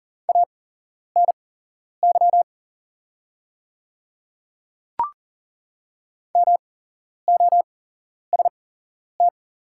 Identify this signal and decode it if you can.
Morse code, heard as a keyed tone, about 40 wpm, spaced as for 11 wpm, 700 Hz. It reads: ANY MOST
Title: 8x Character Spacing - Top 200 Words - 40wpm